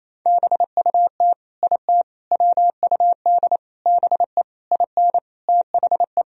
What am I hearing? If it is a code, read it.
Morse code, heard as a keyed tone, about 28 wpm, 700 Hz. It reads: BUT IT WUD BE IN THE
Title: Sentences from Top 300 Words and Abbreviations 28wpm